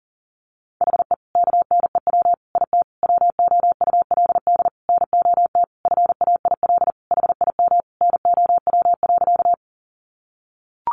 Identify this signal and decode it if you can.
Morse code, heard as a keyed tone, about 40 wpm, 700 Hz. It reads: HE KNEW IT WOULD NOT FAIL HIM NOW.
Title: Sentences from Top 1100 Words 40wpm